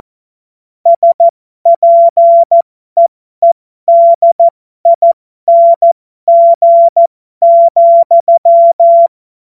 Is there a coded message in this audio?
Yes: Morse code from 0.9 to 9.1 s, heard as a keyed tone, about 14 wpm, spaced as for 12 wpm, 680 Hz. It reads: SPEEDING,